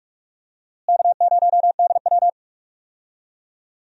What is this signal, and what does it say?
Morse code, heard as a keyed tone, about 41 wpm, 700 Hz. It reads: K0DW
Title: USA Call Signs - 4 Characters - Repeated 3x - 45wpm